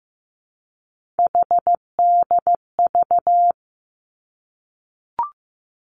Morse code, heard as a keyed tone, about 15 wpm, 700 Hz. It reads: HDV
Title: Three Challenging Letter-Number Combinations 15wpm